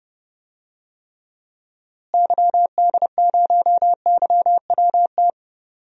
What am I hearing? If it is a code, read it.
Morse code, heard as a keyed tone, about 30 wpm, 700 Hz. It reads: YD0YWT
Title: DXCC-100 Call Signs - 6 Characters - Repeated 3x - 30wpm